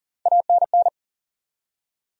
Morse code, heard as a keyed tone, about 38 wpm, 700 Hz. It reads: ANN
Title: USA Common Names - Max 5 Letters - 40wpm